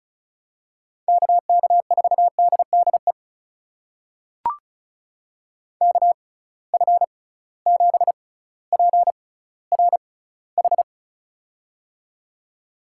Morse code, heard as a keyed tone, about 36 wpm, spaced as for 18 wpm, 700 Hz. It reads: KK4DDE KF7PRH
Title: USA Call Signs - 6x Character Spacing - 35wpm